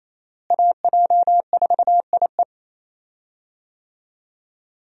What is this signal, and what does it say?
Morse code, heard as a keyed tone, about 28 wpm, 700 Hz. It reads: AJ4IE
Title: USA Call Signs - 5 Characters - Repeated 3x - 28wpm